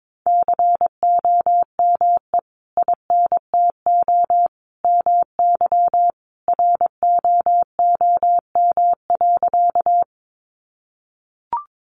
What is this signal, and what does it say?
Morse code, heard as a keyed tone, about 22 wpm, 700 Hz. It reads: COME INTO MY ROOM.
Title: Sentences from Top 600 Words 22wpm